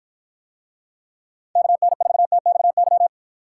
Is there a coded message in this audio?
Yes: Morse code from 1.6 to 3.1 s, heard as a keyed tone, about 49 wpm, 700 Hz. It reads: KN4TXY